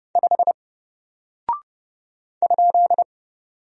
Morse code, heard as a keyed tone, about 30 wpm, 700 Hz. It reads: 5 ?
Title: Single Character (Letter:Number:Punctuation:Prosign) Alt 30wpm